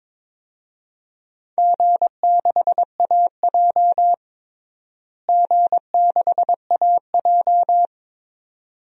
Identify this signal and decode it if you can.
Morse code, heard as a keyed tone, about 22 wpm, 700 Hz. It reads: G6AJ G6AJ
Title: DXCC-100 Call Signs - 4 Characters - Repeated 2x - 22wpm